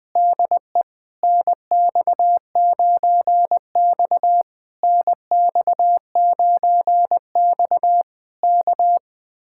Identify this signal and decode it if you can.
Morse code, heard as a keyed tone, about 20 wpm, 700 Hz. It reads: DE NX9X NX9X K